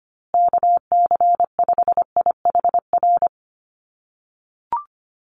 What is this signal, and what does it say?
Morse code, heard as a keyed tone, about 25 wpm, 700 Hz. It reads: KC5IHR
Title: USA Call Signs Repeated 3x 25wpm